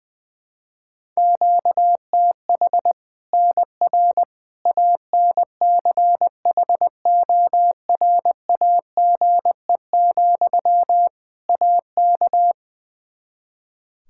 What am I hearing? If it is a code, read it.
Morse code, heard as a keyed tone, about 21 wpm, 700 Hz. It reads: QTH NR ANCHORAGE, AK
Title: K1USN SST QSO Practice 20wpm